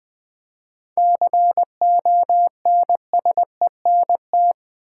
Morse code, heard as a keyed tone, about 20 wpm, 700 Hz. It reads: CONSENT
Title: 7 Letter Words - First 3 Letters Given - 20wpm